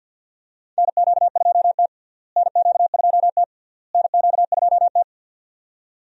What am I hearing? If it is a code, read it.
Morse code, heard as a keyed tone, about 53 wpm, 700 Hz. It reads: NQ2T NQ2T NQ2T
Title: USA Call Signs - 4 Characters - Repeated 3x - 50wpm